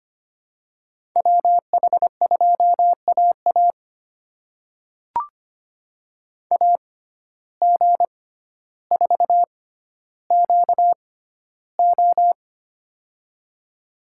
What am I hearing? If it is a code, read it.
Morse code, heard as a keyed tone, about 25 wpm, spaced as for 13 wpm, 700 Hz. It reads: WH2AA AG4QO
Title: USA Call Signs - 5 Characters - 6x Character Spacing - 25wpm